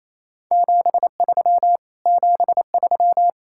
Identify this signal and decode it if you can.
Morse code, heard as a keyed tone, about 38 wpm, 700 Hz. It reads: 73 73